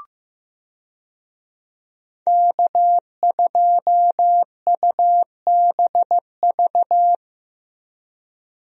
Morse code, heard as a keyed tone, about 15 wpm, 700 Hz. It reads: K2UBV
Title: USA Call Signs - 5 Characters - 15wpm